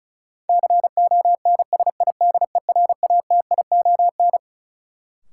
Morse code, heard as a keyed tone, about 36 wpm, 700 Hz. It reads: CONSIDERATION
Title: Sentences from Top 1500 Words - New Word Drill - 35wpm